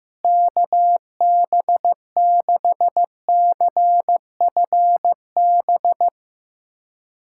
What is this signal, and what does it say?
Morse code, heard as a keyed tone, about 15 wpm, 700 Hz. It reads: KB6CFB
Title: USA Callsign DE Callsign 15wpm